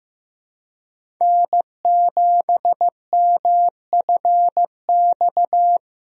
Morse code, heard as a keyed tone, about 15 wpm, 700 Hz. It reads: N7MFX
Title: USA Call Signs - 5 Characters - Repeated 2x - 15wpm